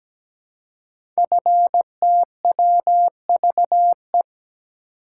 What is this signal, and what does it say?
Morse code, heard as a keyed tone, about 17 wpm, 700 Hz. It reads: FTWVE
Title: Code Groups - 5 Letters - 17wpm